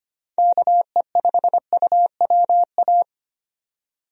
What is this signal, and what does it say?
Morse code, heard as a keyed tone, about 25 wpm, 700 Hz. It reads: KE5UWA
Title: USA Call Signs - 6 Characters - 25wpm